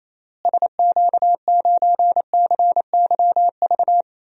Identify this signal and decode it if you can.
Morse code, heard as a keyed tone, about 28 wpm, 700 Hz. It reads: SQ9CYV
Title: DXCC-100 Call Signs - 6 Characters - Repeated 3x - 28wpm